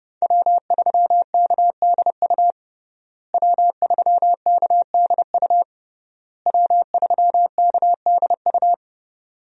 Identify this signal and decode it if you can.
Morse code, heard as a keyed tone, about 30 wpm, 700 Hz. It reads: W3KDU W3KDU W3KDU
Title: USA Call Signs - 5 Characters - Repeated 3x - 30wpm